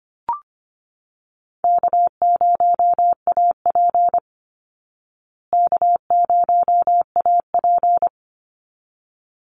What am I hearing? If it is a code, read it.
Morse code, heard as a keyed tone, about 25 wpm, 700 Hz. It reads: K0AP K0AP
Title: Easy Call Signs Repeated 2x - 25wpm